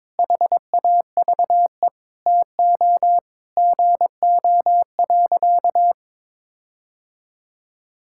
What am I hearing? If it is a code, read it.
Morse code, heard as a keyed tone, about 22 wpm, 700 Hz. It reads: HAVE TO GO.
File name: practice / 22wpm / Sentences with 4-Word Phrases - 22wpm